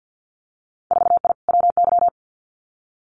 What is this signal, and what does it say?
Morse code, heard as a keyed tone, about 50 wpm, 700 Hz. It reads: 4I WY
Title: Winter Field Day Exchanges Encore 50wpm